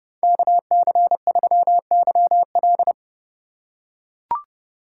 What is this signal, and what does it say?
Morse code, heard as a keyed tone, about 29 wpm, 700 Hz. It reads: KC3YL